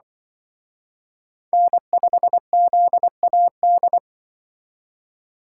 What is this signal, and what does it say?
Morse code, heard as a keyed tone, about 23 wpm, 700 Hz. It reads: N5ZAD
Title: Speed Racing - USA 1x3 Call Signs 20wpm